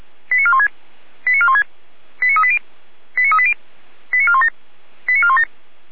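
Five-tone selective call (ZVEI), heard as quick runs of five tones, tones of about 70 ms. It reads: 89416 89416 89380 89380 89417 89417
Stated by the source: ZVEI five-tone call received by SDR, recorded by Feuerwehr Orlamünde (f85360).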